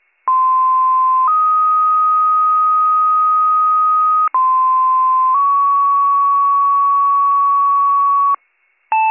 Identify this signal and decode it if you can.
Two-tone page, heard as a one-second tone, then a three-second tone, about 1040 then 1250 Hz, about 1040 then 1120 Hz. It is heard from 0.3 to 8.3 s.